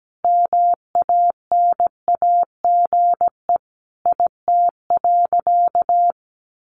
Morse code, heard as a keyed tone, about 17 wpm, 700 Hz. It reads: MANAGE IT.
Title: Sentences from Top 1400 Words 17wpm